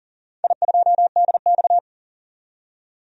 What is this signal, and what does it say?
Morse code, heard as a keyed tone, about 40 wpm, 700 Hz. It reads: I2DX
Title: DXCC-100 Call Signs - 4 Characters - 40wpm